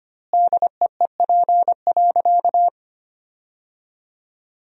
Morse code, heard as a keyed tone, about 25 wpm, 700 Hz. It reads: DEEP.
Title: 2x Word-Spacing - Sentences from Top 500 Words - 25wpm